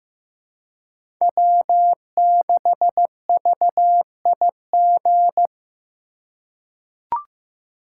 Morse code, heard as a keyed tone, about 15 wpm, 700 Hz. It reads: W6VIG
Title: USA Call Signs - 5 Characters - 4x Character Spacing - 15wpm